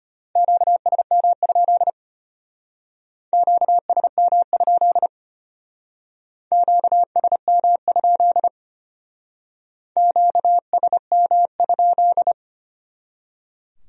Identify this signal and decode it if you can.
Morse code, heard as a keyed tone, about 32 wpm, 700 Hz. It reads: QSM? QSM? QSM? QSM?